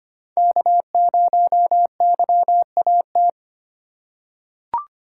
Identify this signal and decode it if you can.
Morse code, heard as a keyed tone, about 25 wpm, 700 Hz. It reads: K0YAT
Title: USA Call Signs - 5 Characters - 6x Character Spacing - 25wpm